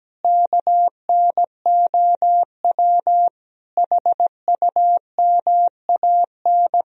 Morse code, heard as a keyed tone, about 17 wpm, 700 Hz. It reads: KNOW HUMAN
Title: Sentences from Top 1000 Words - Review Part 3 - 17wpm